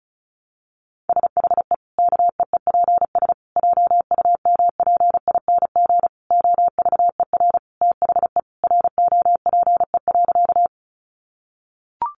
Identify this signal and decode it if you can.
Morse code, heard as a keyed tone, about 34 wpm, 700 Hz. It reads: SHE KEEPS JUMPING OVER THE ROPE.
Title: Easy Phrases 35wpm